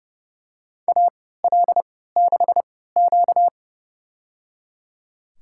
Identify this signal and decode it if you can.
Morse code, heard as a keyed tone, about 30 wpm, spaced as for 17 wpm, 700 Hz. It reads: AL6Q
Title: USA Call Signs - 4 Characters - 3x Character Spacing - 30wpm